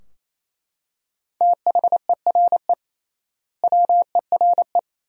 Morse code, heard as a keyed tone, about 28 wpm, 700 Hz. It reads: THERE WERE
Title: 3x Word-Spacing - Sentences from Top 100 Words - 28wpm